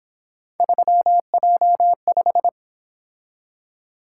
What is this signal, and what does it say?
Morse code, heard as a keyed tone, about 28 wpm, 700 Hz. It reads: 3J5
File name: Speed-Racing - Three Letter-Number Combinations - 22wpm